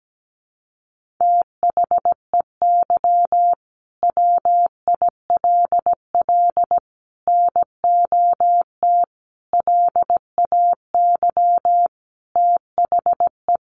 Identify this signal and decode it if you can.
Morse code, heard as a keyed tone, about 17 wpm, 700 Hz. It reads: THEY WILL NOT LAY THE
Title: Sentences from Top 1500 Words 17wpm